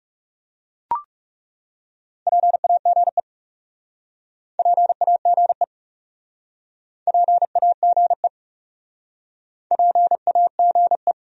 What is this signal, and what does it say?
Morse code, heard as a keyed tone, about 39 wpm, 700 Hz. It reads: PAGE PAGE PAGE PAGE